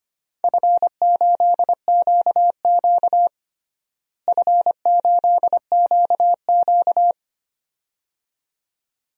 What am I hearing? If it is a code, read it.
Morse code, heard as a keyed tone, about 25 wpm, 700 Hz. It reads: F8QQ F8QQ
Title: DXCC-100 Call Signs - 4 Characters - Repeated 2x - 25wpm